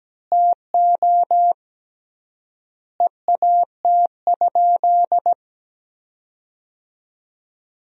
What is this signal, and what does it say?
Morse code, heard as a keyed tone, about 17 wpm, 700 Hz. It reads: TO EAT?
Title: 3x Word-Spacing - Sentences from Top 300 Words - 17wpm